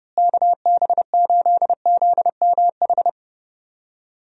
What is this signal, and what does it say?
Morse code, heard as a keyed tone, about 30 wpm, 700 Hz. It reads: KB8ZMH
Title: USA Call Signs - 6 Characters - Repeated 3x - 30wpm